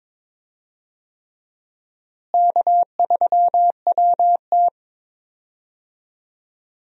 Morse code, heard as a keyed tone, about 21 wpm, 700 Hz. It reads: K3WT
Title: USA Call Signs - 4 Characters - Repeated 3x - 22wpm